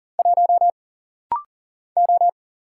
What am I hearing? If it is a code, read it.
Morse code, heard as a keyed tone, about 40 wpm, 700 Hz. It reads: 1 O
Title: Single Letter-Number - Rapid-Fire 40wpm